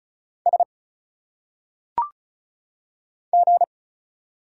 Morse code, heard as a keyed tone, about 34 wpm, 700 Hz. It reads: S G
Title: Single Letters - Phonetic Pronunciation 35wpm